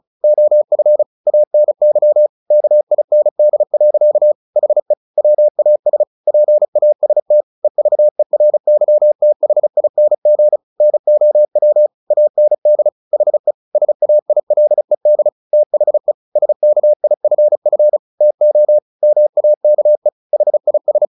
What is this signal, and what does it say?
Morse code, heard as a keyed tone, about 35 wpm, 600 Hz. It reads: OF ANY KIND. HE WAS PAST EVERYTHING NOW AND HE SAILED THE SKIFF TO MAKE HIS
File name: the old man and the sea 2 35wpm